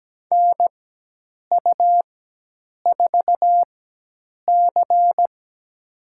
Morse code, heard as a keyed tone, about 17 wpm, spaced as for 8 wpm, 700 Hz. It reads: NU4C
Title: USA Call Signs - 4 Characters - 4x Character Spacing - 17wpm